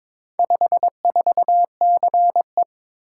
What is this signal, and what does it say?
Morse code, heard as a keyed tone, about 22 wpm, 700 Hz. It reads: 54CE
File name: practice / 22wpm / Four Letter-Number Combinations - 22wpm